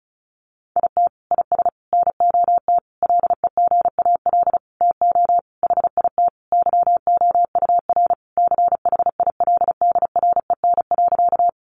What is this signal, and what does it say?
Morse code, heard as a keyed tone, about 34 wpm, 700 Hz. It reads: IT IS NOT LEGAL TO HIT YOUR CHILDREN.
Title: Sentences from Top 1200 Words 35wpm